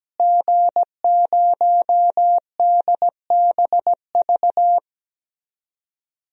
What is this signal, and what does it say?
Morse code, heard as a keyed tone, about 17 wpm, 700 Hz. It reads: G0DBV
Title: DXCC-100 Call Signs - 5 Characters - Repeated 2x - 17wpm